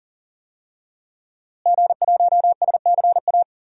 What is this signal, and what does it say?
Morse code, heard as a keyed tone, about 40 wpm, 700 Hz. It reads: G1SCA
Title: DXCC-100 Call Signs - 5 Characters - Repeated 3x - 40wpm